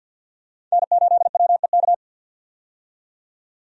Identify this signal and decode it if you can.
Morse code, heard as a keyed tone, about 51 wpm, 700 Hz. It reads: N8WEK
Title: USA Call Signs - 5 Characters - 50wpm